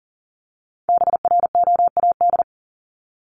Morse code, heard as a keyed tone, about 40 wpm, 700 Hz. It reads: BROAD
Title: Top 751-1000 Words 40wpm